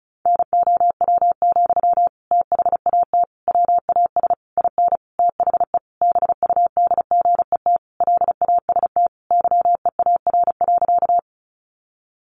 Morse code, heard as a keyed tone, about 34 wpm, 700 Hz. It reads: NOW, THAT WAS IN THE BUDGET LAST YEAR.
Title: Sentences from Top 1200 Words 35wpm